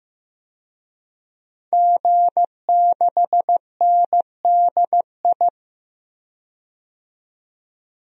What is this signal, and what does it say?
Morse code, heard as a keyed tone, about 15 wpm, 700 Hz. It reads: G6NDI
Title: DXCC-100 Call Signs - 5 Characters - 15wpm